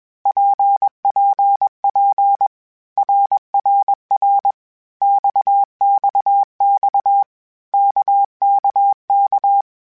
Morse code, heard as a keyed tone, about 21 wpm, 800 Hz. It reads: PPP RRR XXX KKK
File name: Groups@20 / Groups@20 mp3s / RKPX@20